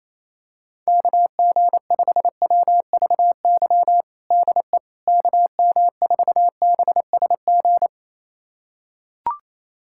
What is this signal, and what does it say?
Morse code, heard as a keyed tone, about 28 wpm, 700 Hz. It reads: KG5WVY DE KM4BSG
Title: USA Callsign DE Callsign - 6 Characters - 28wpm